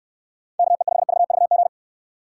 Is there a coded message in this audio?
Yes: Morse code from 0.6 to 1.7 s, heard as a keyed tone, about 69 wpm, 700 Hz. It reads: KE5VVG